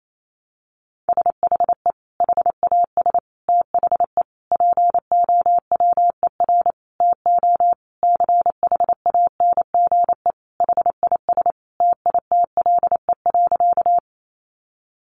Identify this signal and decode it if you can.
Morse code, heard as a keyed tone, about 28 wpm, 700 Hz. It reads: SHE HAS THE POWER TO CHANGE HIS TITLE.
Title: Sentences from Top 1200 Words 28wpm